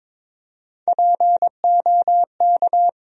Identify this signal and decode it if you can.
Morse code, heard as a keyed tone, about 22 wpm, 700 Hz. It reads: POK
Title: USA Call Sign Postfixes - 3 Letters 22wpm